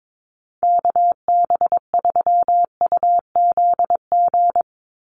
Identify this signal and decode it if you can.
Morse code, heard as a keyed tone, about 22 wpm, 700 Hz. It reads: KB3UZG
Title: USA Call Signs 22wpm